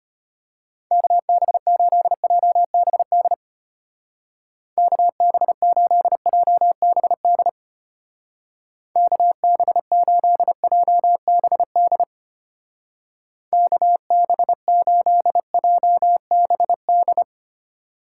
Morse code, heard as a keyed tone, about 32 wpm, 700 Hz. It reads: KB8JBD KB8JBD KB8JBD KB8JBD